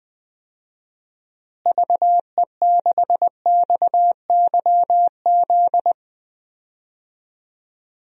Morse code, heard as a keyed tone, about 20 wpm, 700 Hz. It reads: VE6XYZ
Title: DXCC-100 Call Signs - 6 Characters - Repeated 3x - 20wpm